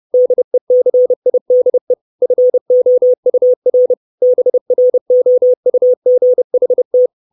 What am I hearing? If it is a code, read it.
Morse code, heard as a keyed tone, about 30 wpm, 500 Hz. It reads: DECIDE FOUR BROUGHT
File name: 632 Words 30WPM